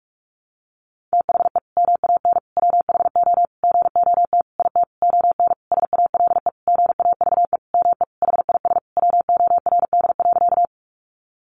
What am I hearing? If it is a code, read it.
Morse code, heard as a keyed tone, about 46 wpm, 700 Hz. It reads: THE MAN WHO GOT IT ON SALE GAVE ME HIS WORD.
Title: Sentences from Top 1400 Words 45wpm